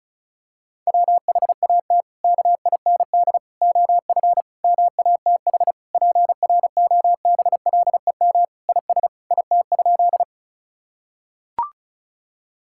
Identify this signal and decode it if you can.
Morse code, heard as a keyed tone, about 34 wpm, 700 Hz. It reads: WHAT KIND OF MATH PROBLEM IS IT?
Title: Sentences from Top 1000 Words - Review Part 6 - 35wpm